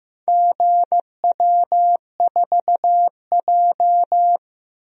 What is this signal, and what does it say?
Morse code, heard as a keyed tone, about 15 wpm, 700 Hz. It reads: GW4J
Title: DXCC-100 Call Signs - 4 Characters - Repeated 2x - 15wpm